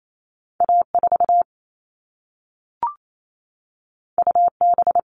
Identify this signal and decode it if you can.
Morse code, heard as a keyed tone, about 28 wpm, 700 Hz. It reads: A4 UB
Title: All Cut Numbers - 2-Digit 28wpm